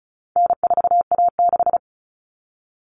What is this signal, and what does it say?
Morse code, heard as a keyed tone, about 34 wpm, 700 Hz. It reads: N4A6
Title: All Cut Numbers - 4-Digit 35wpm